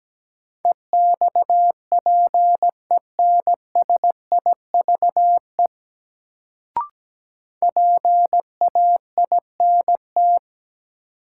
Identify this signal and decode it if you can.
Morse code, heard as a keyed tone, about 17 wpm, 700 Hz. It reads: EXPENSIVE PAINT
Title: Sentences from Top 1000 Words - Review Part 6 - New Word Drill - 17wpm